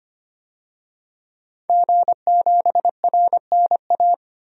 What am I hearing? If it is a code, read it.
Morse code, heard as a keyed tone, about 25 wpm, 700 Hz. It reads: G7RNA
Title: DXCC-100 Call Signs - 5 Characters - 25wpm